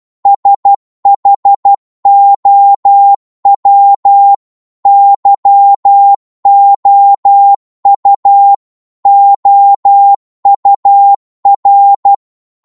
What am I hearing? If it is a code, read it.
Morse code, heard as a keyed tone, about 12 wpm, 800 Hz. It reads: SHOW YOU OUR